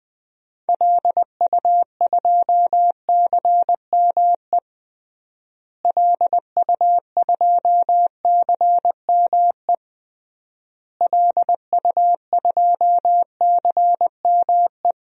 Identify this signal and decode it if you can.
Morse code, heard as a keyed tone, about 20 wpm, 700 Hz. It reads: LU2CME LU2CME LU2CME
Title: DXCC-100 Call Signs - 6 Characters - Repeated 3x - 20wpm